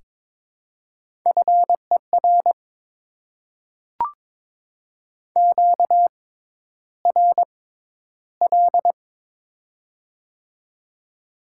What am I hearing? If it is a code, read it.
Morse code, heard as a keyed tone, about 22 wpm, spaced as for 11 wpm, 690 Hz. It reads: FER QRL
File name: Top 20 QSO Elements - 6x Character Spacing - 22wpm